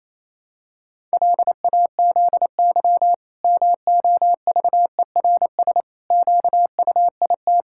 Morse code, heard as a keyed tone, about 28 wpm, 700 Hz. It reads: LAZY MOVERS QUIT
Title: Pangram Challenge v2 - 28wpm